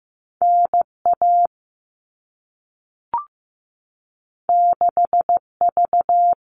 Morse code, heard as a keyed tone, about 15 wpm, 700 Hz. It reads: NA 6V
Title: All Cut Numbers - 2-Digit 15wpm